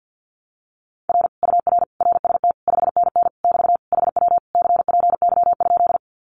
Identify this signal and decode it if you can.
Morse code, heard as a keyed tone, about 48 wpm, 700 Hz. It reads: R UR RST 5NN <BT> HW CPY?
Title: Common QSO Phrases Encore 50wpm